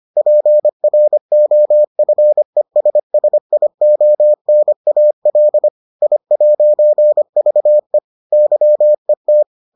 Morse code, heard as a keyed tone, about 25 wpm, 600 Hz. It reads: PROFESSIONAL I'VE YET